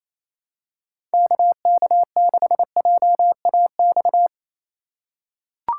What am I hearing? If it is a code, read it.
Morse code, heard as a keyed tone, about 28 wpm, 700 Hz. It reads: KK6JAX
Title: USA Call Signs - 6 Characters - 4x Character Spacing - 28wpm